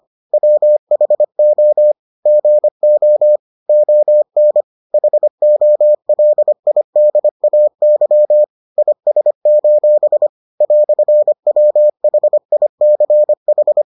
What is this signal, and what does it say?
Morse code, heard as a keyed tone, about 25 wpm, 600 Hz. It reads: WHO GO ON HOLIDAY IS: "WHICH